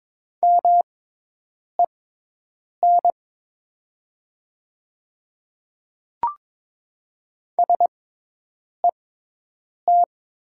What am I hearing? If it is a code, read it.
Morse code, heard as a keyed tone, about 22 wpm, spaced as for 8 wpm, 700 Hz. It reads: MEN SET
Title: 6x Character Spacing - Top 500 Words - 22wpm